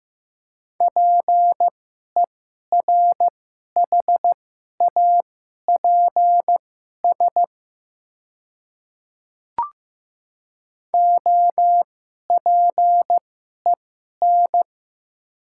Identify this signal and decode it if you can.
Morse code, heard as a keyed tone, about 15 wpm, spaced as for 11 wpm, 700 Hz. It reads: PERHAPS OPEN
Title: 2x Character Spacing - Top 500 Words - 15wpm